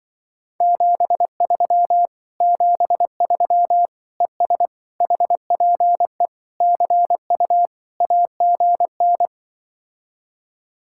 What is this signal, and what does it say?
Morse code, heard as a keyed tone, about 25 wpm, 700 Hz. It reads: 73 73 ES HPE CU AGN